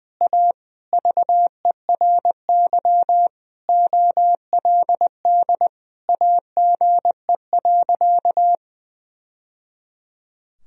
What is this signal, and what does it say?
Morse code, heard as a keyed tone, about 20 wpm, 700 Hz. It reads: A VERY OLD AGE.
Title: Sentences from Top 1000 Words - Review Part 3 - 20wpm